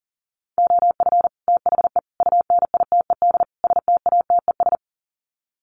Morse code, heard as a keyed tone, about 40 wpm, 700 Hz. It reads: OF THE UNITED STATES
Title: Sets of 4 Words 40wpm